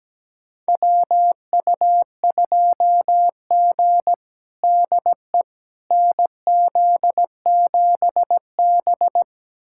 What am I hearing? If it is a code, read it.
Morse code, heard as a keyed tone, about 17 wpm, 700 Hz. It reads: WU2G DE NZ7B
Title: USA Callsign DE Callsign - 4 Characters - 17wpm